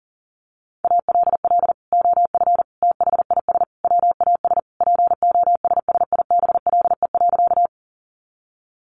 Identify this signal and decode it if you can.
Morse code, heard as a keyed tone, about 40 wpm, 700 Hz. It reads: ALL OF THIS WAS POSSIBLE.
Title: Sentences from Top 500 Words 40wpm